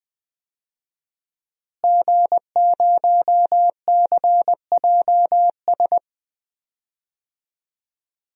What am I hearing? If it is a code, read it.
Morse code, heard as a keyed tone, about 20 wpm, 700 Hz. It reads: G0CJS
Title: DXCC-100 Call Signs - 5 Characters - Repeated 2x - 20wpm